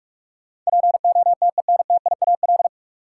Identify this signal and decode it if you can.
Morse code, heard as a keyed tone, about 44 wpm, 700 Hz. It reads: POTENTIAL
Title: Sentences from Top 1000 Words - Review Part 4 - New Word Drill - 45wpm